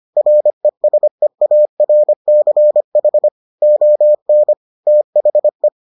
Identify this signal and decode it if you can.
Morse code, heard as a keyed tone, about 25 wpm, 600 Hz. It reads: RESEARCH ON THE